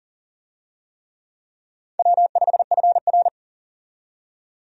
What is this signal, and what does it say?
Morse code, heard as a keyed tone, about 40 wpm, 700 Hz. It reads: W5FR
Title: USA Call Signs - 4 Characters - 40wpm